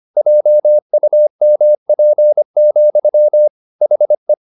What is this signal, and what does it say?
Morse code, heard as a keyed tone, about 25 wpm, 600 Hz. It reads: JUMP, HE
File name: the old man and the sea part 1 25wpm